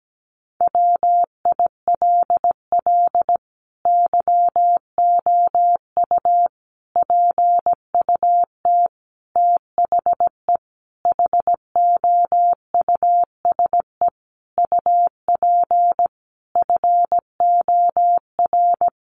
Morse code, heard as a keyed tone, about 17 wpm, 700 Hz. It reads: WILL YOU PUT THE HOUSE UP FOR
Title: Sentences from Top 1400 Words 17wpm